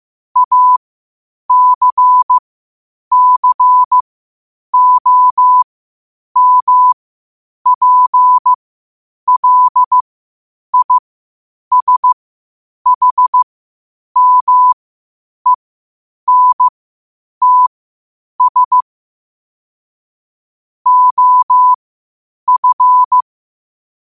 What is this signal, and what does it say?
Morse code, heard as a keyed tone, about 15 wpm, spaced as for 8 wpm, 1000 Hz. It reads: ACCOMPLISHMENTS OF
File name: qotdfiles / 8wpm